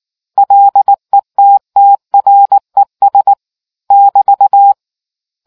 Morse code, heard as a keyed tone, about 19 wpm, 800 Hz. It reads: LETTRES <BT>